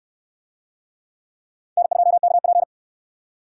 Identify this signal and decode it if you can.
Morse code, heard as a keyed tone, about 65 wpm, 700 Hz. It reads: N2KP